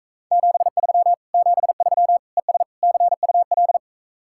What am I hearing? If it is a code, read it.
Morse code, heard as a keyed tone, about 45 wpm, 700 Hz. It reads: 73 73 ES CUL